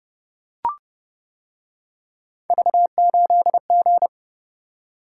Morse code, heard as a keyed tone, about 29 wpm, 700 Hz. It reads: V8G